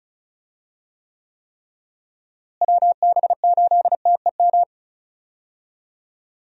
Morse code, heard as a keyed tone, about 34 wpm, 700 Hz. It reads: WB8TEM
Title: USA Call Signs - 6 Characters - Repeated 3x - 35wpm